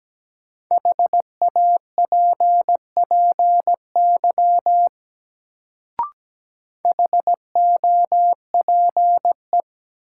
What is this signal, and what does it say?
Morse code, heard as a keyed tone, about 17 wpm, 700 Hz. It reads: HAPPY HOPE
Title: Sentences from Top 1000 Words - New Word Drill - 17wpm